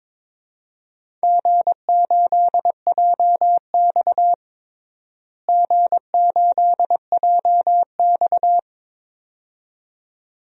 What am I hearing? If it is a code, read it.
Morse code, heard as a keyed tone, about 22 wpm, 700 Hz. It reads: G8JX G8JX
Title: DXCC-100 Call Signs - 4 Characters - Repeated 2x - 22wpm